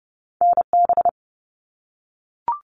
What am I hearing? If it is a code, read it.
Morse code, heard as a keyed tone, about 30 wpm, 700 Hz. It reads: NB
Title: Canadian Provinces and Territories 30wpm